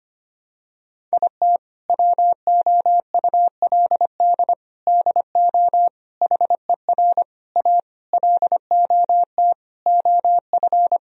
Morse code, heard as a keyed tone, about 25 wpm, 700 Hz. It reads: IT WOULD DO HER A LOT OF
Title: Sentences from Top 1000 Words - Review Part 3 - 25wpm